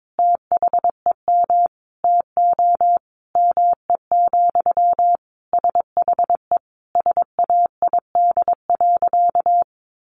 Morse code, heard as a keyed tone, about 22 wpm, 700 Hz. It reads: THEM TO ME, SHE SAID.
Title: Sentences from Top 500 Words 22wpm